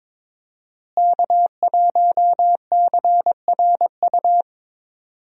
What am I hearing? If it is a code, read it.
Morse code, heard as a keyed tone, about 22 wpm, 700 Hz. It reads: K1CRU
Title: USA Call Signs - 5 Characters - Repeated 3x - 22wpm